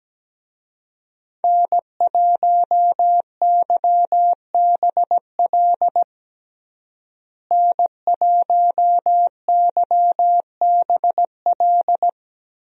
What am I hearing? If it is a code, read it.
Morse code, heard as a keyed tone, about 17 wpm, 700 Hz. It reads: N1YBL N1YBL
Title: USA Call Signs - 5 Characters - Repeated 2x - 17wpm